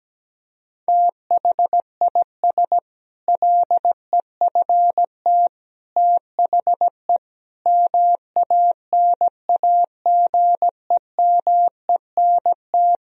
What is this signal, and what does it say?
Morse code, heard as a keyed tone, about 17 wpm, 700 Hz. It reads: THIS LEFT THE MANAGEMENT
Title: Sentences from Top 1000 Words - Review Part 4 - 17wpm